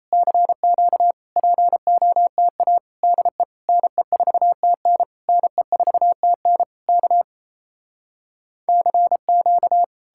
Morse code, heard as a keyed tone, about 38 wpm, 700 Hz. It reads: CQ POTA DE NE4TN NE4TN K CQ